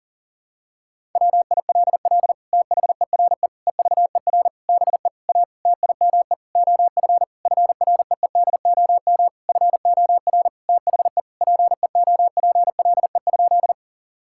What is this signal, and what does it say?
Morse code, heard as a keyed tone, about 40 wpm, 700 Hz. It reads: WILL THERE EVER BE A TIME OF FREEDOM FOR THE PEOPLE?
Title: Sentences from Top 1000 Words - Review Part 5 - 40wpm